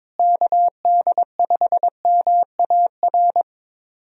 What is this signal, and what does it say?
Morse code, heard as a keyed tone, about 22 wpm, 700 Hz. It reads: KD5MAR
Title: USA Call Signs - 6 Characters - Repeated 3x - 22wpm